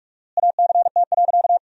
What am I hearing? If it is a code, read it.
Morse code, heard as a keyed tone, about 47 wpm, 700 Hz. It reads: ACT.
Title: 4x Word-Spacing - Sentences from Top 200 Words - 45wpm